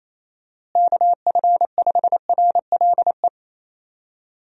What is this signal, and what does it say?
Morse code, heard as a keyed tone, about 27 wpm, 700 Hz. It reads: KF5RLE
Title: USA Call Signs - 6 Characters - Repeated 3x - 28wpm